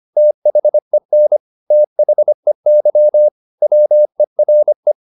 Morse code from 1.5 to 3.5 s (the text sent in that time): THEY